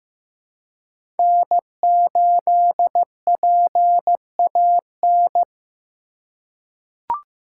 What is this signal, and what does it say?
Morse code, heard as a keyed tone, about 15 wpm, 700 Hz. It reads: N8PAN
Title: USA Call Signs - 5 Characters - 2x Character Spacing - 15wpm